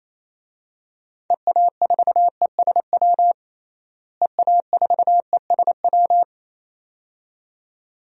Morse code, heard as a keyed tone, about 28 wpm, 700 Hz. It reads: EA4ESW EA4ESW
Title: DXCC-100 Call Signs - 6 Characters - Repeated 2x - 28wpm